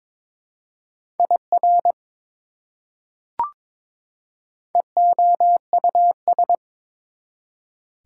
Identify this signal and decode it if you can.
Morse code, heard as a keyed tone, about 22 wpm, 700 Hz. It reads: IR EOUS
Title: Prefixes and Suffixes 22wpm